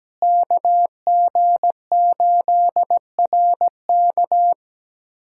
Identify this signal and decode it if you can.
Morse code, heard as a keyed tone, about 17 wpm, 700 Hz. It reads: KG8RK
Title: USA Callsign DE Callsign 17wpm